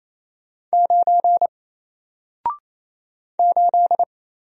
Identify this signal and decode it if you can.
Morse code, heard as a keyed tone, about 28 wpm, 700 Hz. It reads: 9 8
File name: Single-Letter-Number-28wpm